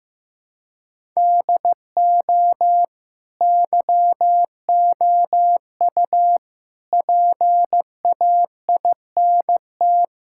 Morse code, heard as a keyed tone, about 15 wpm, 700 Hz. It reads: DO YOU PAINT
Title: Sentences from Top 1000 Words - Review Part 6 - 15wpm